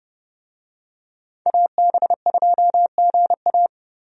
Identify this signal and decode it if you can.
Morse code, heard as a keyed tone, about 30 wpm, 700 Hz. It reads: AB2GA
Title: USA Call Signs - 5 Characters - 30wpm